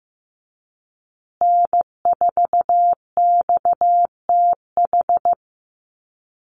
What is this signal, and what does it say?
Morse code, heard as a keyed tone, about 15 wpm, 700 Hz. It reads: N4XTH
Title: USA Call Signs Repeated 3x 15wpm